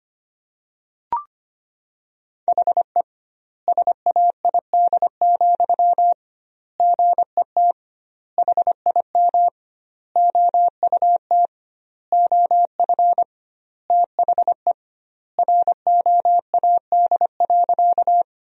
Morse code, heard as a keyed tone, about 25 wpm, 700 Hz. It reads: HE SAID, GET HIM OUT OF THE ROAD.
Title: 2x Word-Spacing - Sentences from Top 400 Words - 25wpm